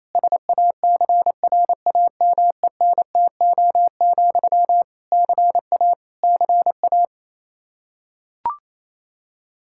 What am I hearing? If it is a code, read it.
Morse code, heard as a keyed tone, about 28 wpm, 700 Hz. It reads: SACRAMENTO, CA CA
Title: K1USN SST QSO Practice 28wpm